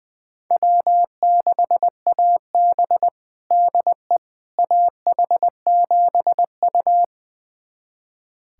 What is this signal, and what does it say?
Morse code, heard as a keyed tone, about 20 wpm, 700 Hz. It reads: W6AB DE AH7U
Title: USA Callsign DE Callsign - 4 Characters - 20wpm